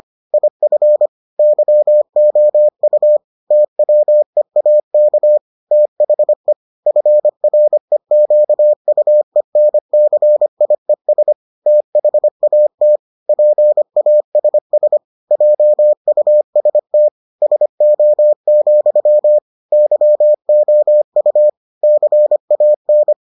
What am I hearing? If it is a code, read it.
Morse code, heard as a keyed tone, about 25 wpm, 600 Hz. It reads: IF YOU TWEAK THE FREQUENCIES THAT PASS JUST SO, YOU CAN